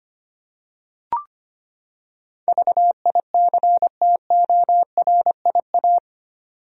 Morse code, heard as a keyed tone, about 25 wpm, 700 Hz. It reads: VICTORIA
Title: USA Common Names - 25wpm